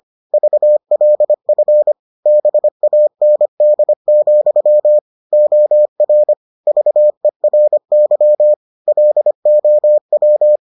Morse code, heard as a keyed tone, about 25 wpm, 600 Hz. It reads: VLF BAND, OR VERY LOW